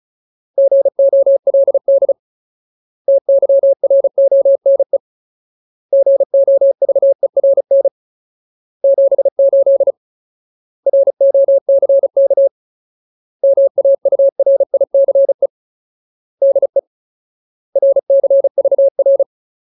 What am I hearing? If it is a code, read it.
Morse code, heard as a keyed tone, about 34 wpm, 550 Hz. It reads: GOLD TYRONE GOVERN 78 ROCK MAURICE DE RCVR
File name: Ultimate Word List / 35WPM Ultimate